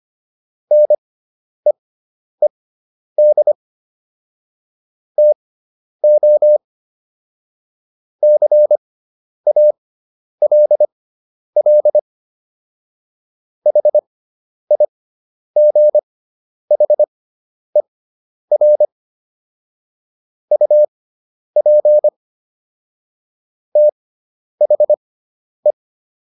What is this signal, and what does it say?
Morse code, heard as a keyed tone, about 25 wpm, spaced as for 10 wpm, 600 Hz. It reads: NEED TO CALL HIGHER UP THE